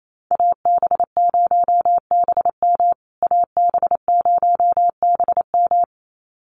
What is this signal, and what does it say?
Morse code, heard as a keyed tone, about 28 wpm, 700 Hz. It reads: AB0BM AB0BM
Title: USA Call Signs Repeated 2x 28wpm